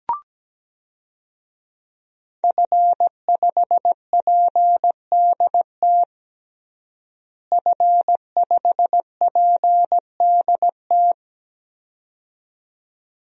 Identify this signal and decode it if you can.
Morse code, heard as a keyed tone, about 17 wpm, 700 Hz. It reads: F5PDT F5PDT
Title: DXCC-100 Call Signs - 5 Characters - Repeated 2x - 17wpm